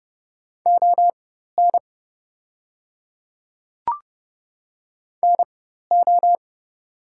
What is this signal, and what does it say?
Morse code, heard as a keyed tone, about 30 wpm, spaced as for 14 wpm, 690 Hz. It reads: ON NO
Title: 4x Character Spacing - Top 100 Words - 30wpm